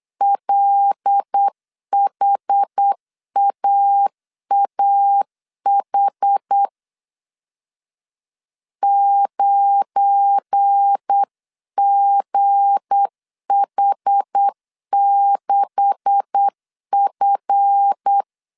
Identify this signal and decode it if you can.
Morse code, heard as a keyed tone, about 8 wpm, 800 Hz. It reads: LHAAH 9GH6F